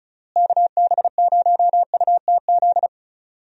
Morse code, heard as a keyed tone, about 36 wpm, 700 Hz. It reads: KB0UTZ
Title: USA Call Signs - 6 Characters - Repeated 3x - 35wpm